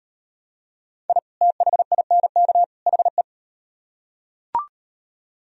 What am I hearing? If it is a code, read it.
Morse code, heard as a keyed tone, about 38 wpm, 700 Hz. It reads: I THINK HE